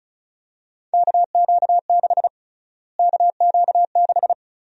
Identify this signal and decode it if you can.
Morse code, heard as a keyed tone, about 34 wpm, 700 Hz. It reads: KQ6 KQ6
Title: USA Call Sign Prefixes - Repeated 2x - 35wpm